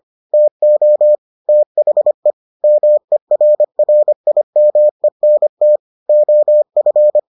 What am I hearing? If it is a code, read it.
Morse code, heard as a keyed tone, about 25 wpm, 600 Hz. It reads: TO THE MERRIMENT OF